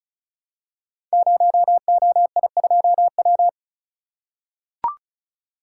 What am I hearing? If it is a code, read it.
Morse code, heard as a keyed tone, about 36 wpm, 700 Hz. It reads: 0OI2W
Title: Code Groups - 5 Letter:Numbers - 35wpm